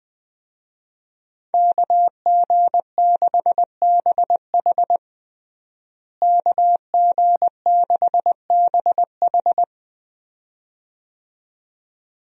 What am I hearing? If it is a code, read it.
Morse code, heard as a keyed tone, about 20 wpm, 700 Hz. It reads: KG6BH KG6BH
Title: USA Call Signs - 5 Characters - Repeated 2x - 20wpm